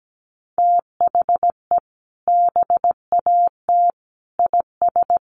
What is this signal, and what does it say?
Morse code, heard as a keyed tone, about 17 wpm, 700 Hz. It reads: THE BAT IS
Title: Sentences from Top 1500 Words 17wpm